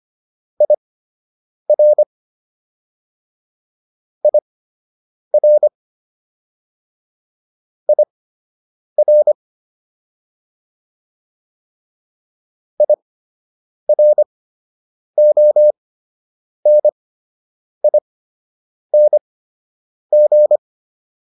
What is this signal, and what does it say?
Morse code, heard as a keyed tone, about 25 wpm, spaced as for 8 wpm, 600 Hz. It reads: IR IR IR IRONING